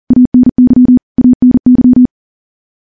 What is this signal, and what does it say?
Morse code, heard as a keyed tone, about 39 wpm, 250 Hz. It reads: ANY ANY
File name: morse / morse-training-40WPM